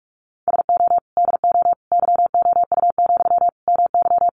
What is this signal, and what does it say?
Morse code, heard as a keyed tone, about 44 wpm, 700 Hz. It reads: SO DO YOU, MY
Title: Sentences from Top 400 Words 45wpm